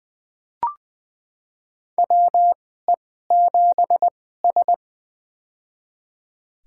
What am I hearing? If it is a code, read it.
Morse code, heard as a keyed tone, about 20 wpm, spaced as for 14 wpm, 700 Hz. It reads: WE7S
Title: USA Call Signs - 4 Characters - 2x Character Spacing - 20wpm